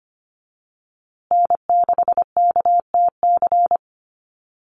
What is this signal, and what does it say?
Morse code, heard as a keyed tone, about 25 wpm, 700 Hz. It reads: N6KTC
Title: USA Call Signs Repeated 3x 25wpm